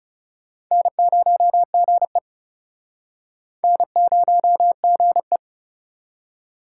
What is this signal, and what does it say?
Morse code, heard as a keyed tone, about 42 wpm, 700 Hz. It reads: N0GE N0GE